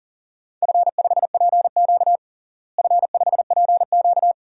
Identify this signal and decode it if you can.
Morse code, heard as a keyed tone, about 40 wpm, 700 Hz. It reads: F5PQ F5PQ
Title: DXCC-100 Call Signs - 4 Characters - Repeated 2x - 40wpm